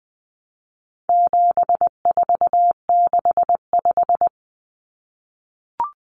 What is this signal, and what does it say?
Morse code, heard as a keyed tone, about 20 wpm, 700 Hz. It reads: 7465
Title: Numbers - 4-Digit 20wpm